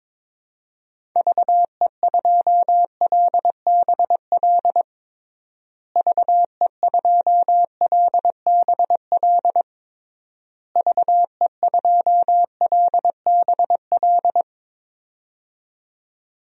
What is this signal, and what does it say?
Morse code, heard as a keyed tone, about 22 wpm, 700 Hz. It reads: VE2LBL VE2LBL VE2LBL
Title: DXCC-100 Call Signs - 6 Characters - Repeated 3x - 22wpm